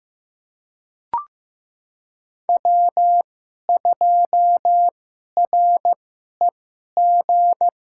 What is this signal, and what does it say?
Morse code, heard as a keyed tone, about 15 wpm, spaced as for 11 wpm, 700 Hz. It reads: W2REG
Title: USA Call Signs - 5 Characters - 2x Character Spacing - 15wpm